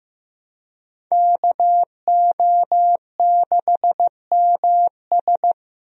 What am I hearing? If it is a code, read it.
Morse code, heard as a keyed tone, about 15 wpm, 700 Hz. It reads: KO6MS
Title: USA Call Signs - 5 Characters - Repeated 2x - 15wpm